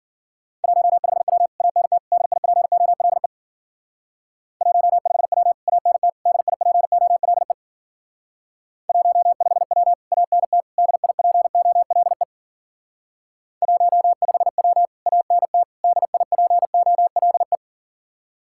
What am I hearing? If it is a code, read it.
Morse code, heard as a keyed tone, about 50 wpm, 700 Hz. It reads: 15W ANT DIPOLE 15W ANT DIPOLE 15W ANT DIPOLE 15W ANT DIPOLE
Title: Speed-Racing Common QSO Phrases - 40wpm